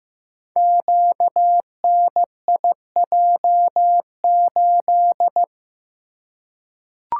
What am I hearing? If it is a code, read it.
Morse code, heard as a keyed tone, about 15 wpm, 700 Hz. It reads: QNIJ8
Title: Code Groups - 5 Letter:Numbers - 15wpm